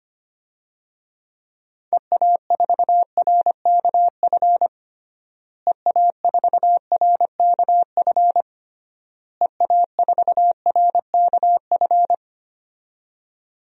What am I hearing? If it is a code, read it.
Morse code, heard as a keyed tone, about 25 wpm, 700 Hz. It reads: EA4RKF EA4RKF EA4RKF
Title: DXCC-100 Call Signs - 6 Characters - Repeated 3x - 25wpm